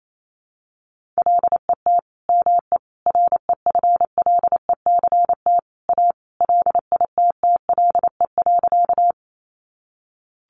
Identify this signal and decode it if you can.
Morse code, heard as a keyed tone, about 28 wpm, 700 Hz. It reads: LET ME REFLECT A LITTLE.
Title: Sentences from Top 1500 Words 28wpm